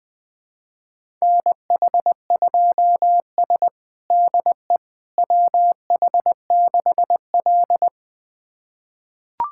Morse code, heard as a keyed tone, about 21 wpm, 700 Hz. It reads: NH2S DE WH6L
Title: USA Callsign DE Callsign - 4 Characters - 20wpm